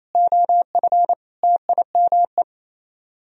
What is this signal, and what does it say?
Morse code, heard as a keyed tone, about 35 wpm, 700 Hz. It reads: OF TIME
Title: Speed-Racing - Sets of 4 Words 28wpm